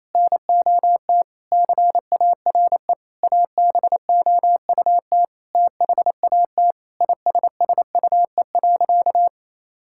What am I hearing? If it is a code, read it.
Morse code, heard as a keyed tone, about 28 wpm, 700 Hz. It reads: NOT CARE ABOUT THAT ISSUE.
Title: Sentences from Top 1000 Words - Review Part 6 - 28wpm